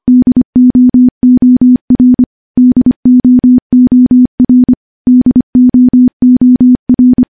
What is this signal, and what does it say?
Morse code, heard as a keyed tone, about 25 wpm, 250 Hz. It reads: DOOR DOOR DOOR
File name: morse-training-25WPM